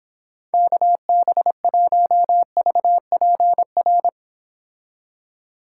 Morse code, heard as a keyed tone, about 31 wpm, 700 Hz. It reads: KB1VPR